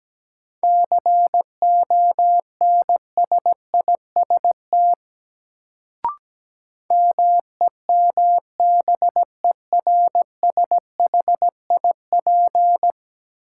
Morse code, heard as a keyed tone, about 17 wpm, 700 Hz. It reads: CONSIST MEMBERSHIP
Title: Sentences from Top 1400 Words - New Word Drill - 17wpm